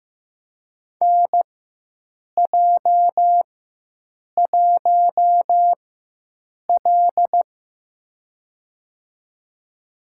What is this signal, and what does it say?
Morse code, heard as a keyed tone, about 15 wpm, spaced as for 7 wpm, 700 Hz. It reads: NJ1L
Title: USA Call Signs - 4 Characters - 4x Character Spacing - 15wpm